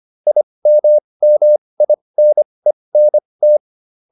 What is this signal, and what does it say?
Morse code, heard as a keyed tone, about 25 wpm, spaced as for 20 wpm, 600 Hz. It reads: IMMINENT